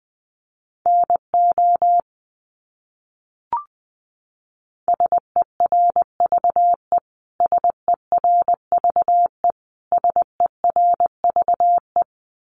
Morse code, heard as a keyed tone, about 20 wpm, 700 Hz. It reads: NO SERVE SERVE SERVE
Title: Top 500 Words Repeated 20wpm